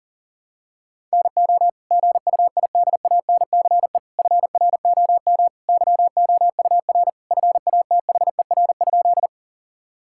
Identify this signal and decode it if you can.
Morse code, heard as a keyed tone, about 40 wpm, 700 Hz. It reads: NO GUIDANCE FROM YOUR FATHER?
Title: Sentences from Top 1600 Words 40wpm